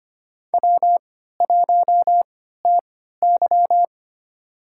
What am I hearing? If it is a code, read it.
Morse code, heard as a keyed tone, about 25 wpm, spaced as for 14 wpm, 700 Hz. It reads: W1TY
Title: USA Call Signs - 3x Character Spacing - 25wpm